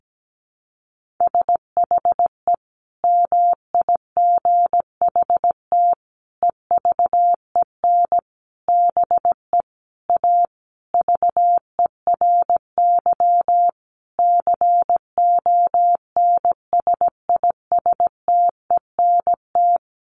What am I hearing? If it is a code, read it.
Morse code, heard as a keyed tone, about 17 wpm, 700 Hz. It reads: SHE MIGHT EVEN BE A VERY CONSISTENT
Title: Sentences from Top 1500 Words 17wpm